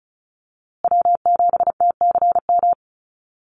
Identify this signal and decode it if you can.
Morse code, heard as a keyed tone, about 34 wpm, 700 Hz. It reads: W7TCM
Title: USA Call Signs Repeated 2x 35wpm